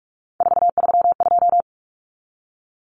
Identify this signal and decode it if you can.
Morse code, heard as a keyed tone, about 44 wpm, 700 Hz. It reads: 432
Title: Numbers - 3-Digit 45wpm